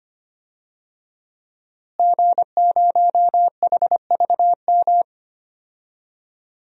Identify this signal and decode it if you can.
Morse code, heard as a keyed tone, about 25 wpm, 700 Hz. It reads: G0HVM
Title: DXCC-100 Call Signs - 5 Characters - Repeated 3x - 25wpm